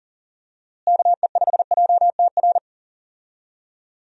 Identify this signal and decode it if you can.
Morse code, heard as a keyed tone, about 39 wpm, 700 Hz. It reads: KE5JTR
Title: USA Call Signs - 6 Characters - Repeated 3x - 40wpm